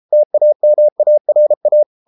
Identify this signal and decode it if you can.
Morse code, heard as a keyed tone, about 33 wpm, 600 Hz. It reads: TAMARA